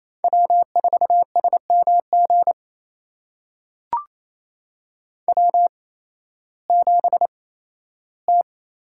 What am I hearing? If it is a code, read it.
Morse code, heard as a keyed tone, about 28 wpm, spaced as for 15 wpm, 700 Hz. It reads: W4SMG W7T
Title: USA Call Signs - 5 Characters - 8x Character Spacing - 28wpm